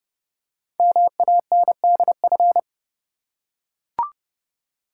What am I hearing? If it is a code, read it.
Morse code, heard as a keyed tone, about 30 wpm, 700 Hz. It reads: MANDF
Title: Code Groups - 5 Letters - 30wpm